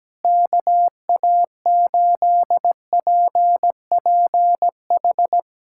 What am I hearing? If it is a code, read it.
Morse code, heard as a keyed tone, about 17 wpm, 700 Hz. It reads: KA8PPH
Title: USA Callsign DE Callsign - 6 Characters - 17wpm